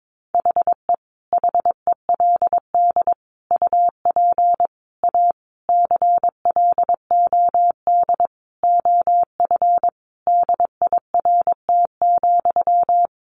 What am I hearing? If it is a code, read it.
Morse code, heard as a keyed tone, about 22 wpm, 700 Hz. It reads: HE HELD UP A CLOD OF DIRT,
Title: A Sound Of Thunder 22wpm